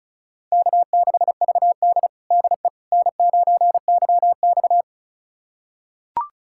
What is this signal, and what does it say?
Morse code, heard as a keyed tone, about 34 wpm, 700 Hz. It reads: K6VD DE N9YX
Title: USA Callsign DE Callsign - 4 Characters - 35wpm